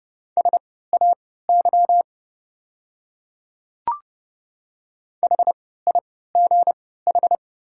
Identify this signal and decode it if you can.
Morse code, heard as a keyed tone, about 32 wpm, spaced as for 17 wpm, 700 Hz. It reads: SAY HIGH
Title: 3x Character Spacing - Top 200 Words - 30wpm